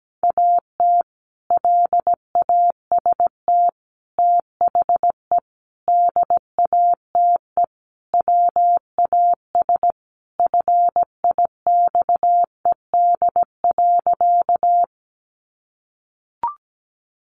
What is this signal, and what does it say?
Morse code, heard as a keyed tone, about 17 wpm, 700 Hz. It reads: AT LAST THE DATE WAS FIXED.
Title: Sentences from Top 1100 Words 17wpm